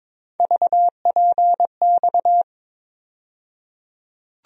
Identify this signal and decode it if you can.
Morse code, heard as a keyed tone, about 22 wpm, 700 Hz. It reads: VPX
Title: USA Call Sign Postfixes - 3 Letters 22wpm